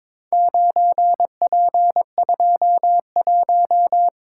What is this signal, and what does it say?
Morse code, heard as a keyed tone, about 21 wpm, 700 Hz. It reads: 9P21
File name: practice / 22wpm / Four Letter-Number Combinations - 22wpm